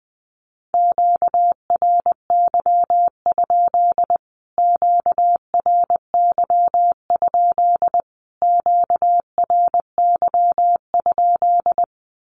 Morse code, heard as a keyed tone, about 20 wpm, 700 Hz. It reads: QRY? QRY? QRY?
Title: Q-Codes Repeated 3x 20wpm